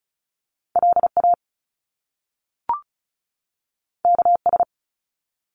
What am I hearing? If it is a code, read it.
Morse code, heard as a keyed tone, about 36 wpm, 700 Hz. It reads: LA KS
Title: US States 35wpm